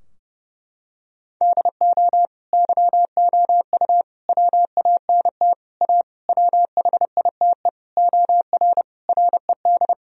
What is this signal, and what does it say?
Morse code, heard as a keyed tone, about 30 wpm, 700 Hz. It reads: DO YOU WANT A WHITE OR RED
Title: Sentences from Top 1000 Words - Review Part 2 - 30wpm